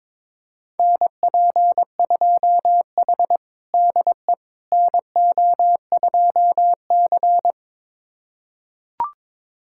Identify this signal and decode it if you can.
Morse code, heard as a keyed tone, about 22 wpm, 700 Hz. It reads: NP2H DE NO2C
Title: USA Callsign DE Callsign - 4 Characters - 22wpm